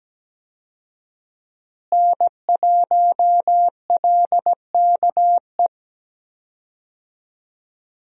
Morse code, heard as a keyed tone, about 17 wpm, 700 Hz. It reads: N1LKE
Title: USA Call Signs - 5 Characters - Repeated 2x - 17wpm